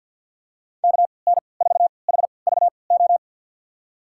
Morse code, heard as a keyed tone, about 51 wpm, spaced as for 28 wpm, 690 Hz. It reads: KN4HVO
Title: USA Call Signs - 6 Characters - 3x Character Spacing - 50wpm